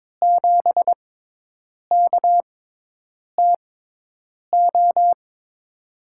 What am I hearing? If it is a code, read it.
Morse code, heard as a keyed tone, about 22 wpm, spaced as for 8 wpm, 700 Hz. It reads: 7KTO
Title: USA Call Signs - 5 Characters - 6x Character Spacing - 22wpm